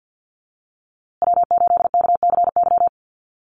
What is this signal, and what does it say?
Morse code, heard as a keyed tone, about 50 wpm, 700 Hz. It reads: W8KCY
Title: USA Call Signs Repeated 3x 50wpm